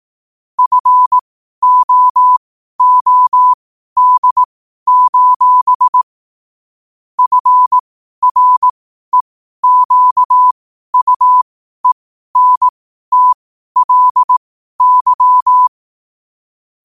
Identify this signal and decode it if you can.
Morse code, heard as a keyed tone, about 18 wpm, spaced as for 12 wpm, 1000 Hz. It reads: FOOD: FREQUENTLY
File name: qotdfiles / 12wpm